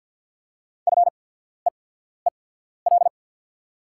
Morse code, heard as a keyed tone, about 48 wpm, spaced as for 14 wpm, 690 Hz. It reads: FEEL
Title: 8x Character Spacing - Top 500 Words - 50wpm